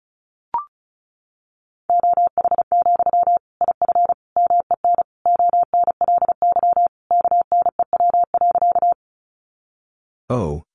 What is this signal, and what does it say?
Morse code, heard as a keyed tone, about 34 wpm, 700 Hz. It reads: OH, IF MEN ONLY KNEW.
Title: Sentences from Top 500 Words 35wpm